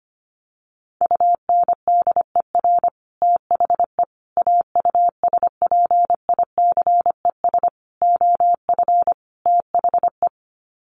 Morse code, heard as a keyed tone, about 25 wpm, 700 Hz. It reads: UNDER THE AUSPICES OF THE
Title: Sets of 5 Words Encore2 25wpm